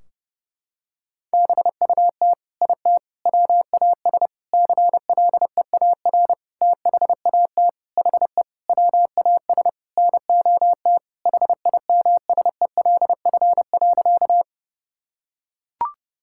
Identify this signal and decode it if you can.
Morse code, heard as a keyed tone, about 32 wpm, 700 Hz. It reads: BUT IT WAS CLEAR THAT HE WAS NOT HIMSELF.
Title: Sentences from Top 1600 Words 30wpm